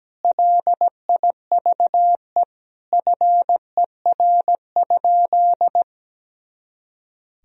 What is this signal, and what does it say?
Morse code, heard as a keyed tone, about 17 wpm, 700 Hz. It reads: LIVE FER?
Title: Sentences from Top 200 Words and Abbreviations 17wpm